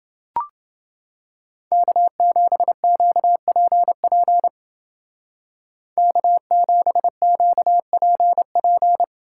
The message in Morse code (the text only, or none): K7QPP K7QPP